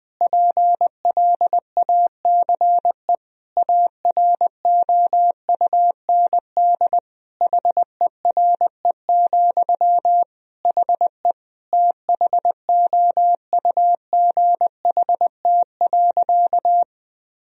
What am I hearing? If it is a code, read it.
Morse code, heard as a keyed tone, about 21 wpm, 700 Hz. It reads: PLACE AROUND HERE, HE THOUGHT.
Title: Sentences from Top 1000 Words - Review Part 3 - 20wpm